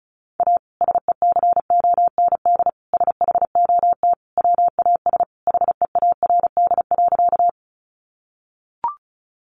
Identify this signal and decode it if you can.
Morse code, heard as a keyed tone, about 34 wpm, 700 Hz. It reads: A SECOND SHOT WAS HEARD.
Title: Sentences from Top 1100 Words 35wpm